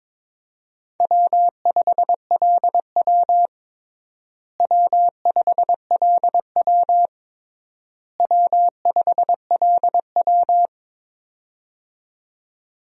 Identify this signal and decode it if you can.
Morse code, heard as a keyed tone, about 22 wpm, 700 Hz. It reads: W5LW W5LW W5LW
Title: USA Call Signs - 4 Characters - Repeated 3x - 22wpm